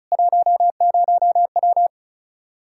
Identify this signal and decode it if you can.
Morse code, heard as a keyed tone, about 36 wpm, 700 Hz. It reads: 10W
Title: Practice QSO - Exchange 2 - Info - 35wpm